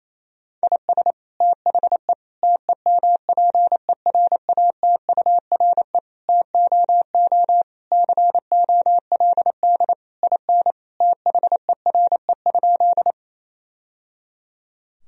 Morse code, heard as a keyed tone, about 28 wpm, 700 Hz. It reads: IS THE TEMPERATURE TOO COLD IN THERE?
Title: Sentences from Top 1000 Words - Review Part 4 - 28wpm